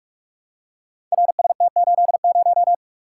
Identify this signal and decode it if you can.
Morse code, heard as a keyed tone, about 44 wpm, 700 Hz. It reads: RST80
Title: Code Groups - 5 Letter:Numbers - 45wpm